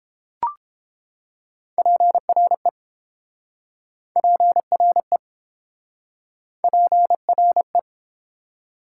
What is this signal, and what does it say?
Morse code, heard as a keyed tone, about 30 wpm, 700 Hz. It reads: PRE PRE PRE